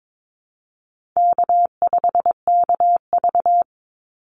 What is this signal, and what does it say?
Morse code, heard as a keyed tone, about 22 wpm, 700 Hz. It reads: K5KV
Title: Call Signs Repeated 2x - 22wpm